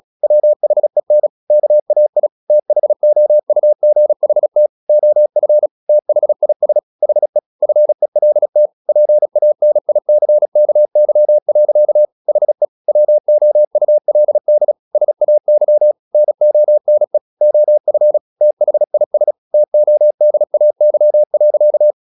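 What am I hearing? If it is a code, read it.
Morse code, heard as a keyed tone, about 36 wpm, 600 Hz. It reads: WHEN KAI THOUGHT OF THIS HE FELT PANICKY. HE WOULD SAY NONE OF THIS TODAY.
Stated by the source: Morse practice at 36 wpm, text of Ken Follett's novel Never, from CW med Gnister.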